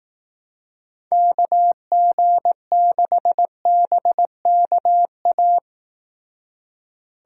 Morse code, heard as a keyed tone, about 18 wpm, 700 Hz. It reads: KG6BKA
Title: Speed Racing - USA 2x3 Call Signs 15wpm